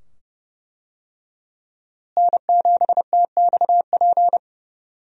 Morse code, heard as a keyed tone, about 32 wpm, 700 Hz. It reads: N7TXP
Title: USA Call Signs - 5 Characters - Repeated 2x - 30wpm